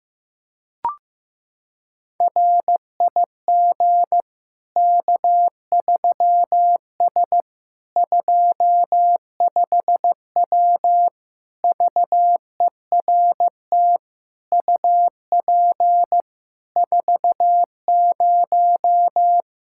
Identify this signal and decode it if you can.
Morse code, heard as a keyed tone, about 16 wpm, 700 Hz. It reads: RIG K3S 25W VERT UP 40
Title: K1USN SST QSO Practice 15wpm